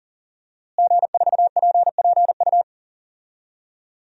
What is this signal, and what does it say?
Morse code, heard as a keyed tone, about 40 wpm, 700 Hz. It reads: G4PPU
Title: DXCC-100 Call Signs - 5 Characters - Repeated 2x - 40wpm